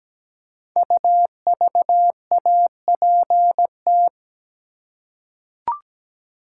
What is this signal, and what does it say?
Morse code, heard as a keyed tone, about 17 wpm, 700 Hz. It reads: UVAPT
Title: Code Groups - 5 Letter:Numbers - 17wpm